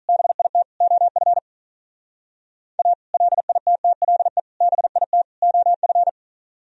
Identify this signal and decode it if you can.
Morse code, heard as a keyed tone, about 43 wpm, 700 Hz. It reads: BIT OF A LITTLE BIT OF
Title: Speed-Racing - Sets of 4 Words 35wpm